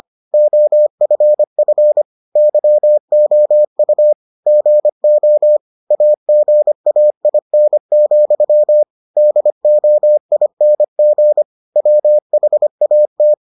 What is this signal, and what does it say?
Morse code, heard as a keyed tone, about 25 wpm, 600 Hz. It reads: OFF YOU GO AGAIN, DOING WHAT